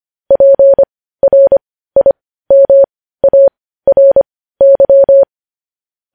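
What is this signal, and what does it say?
Morse code, heard as a keyed tone, about 25 wpm, spaced as for 15 wpm, 550 Hz. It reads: PRIMARY
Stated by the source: News headlines sent as Morse practice at 15.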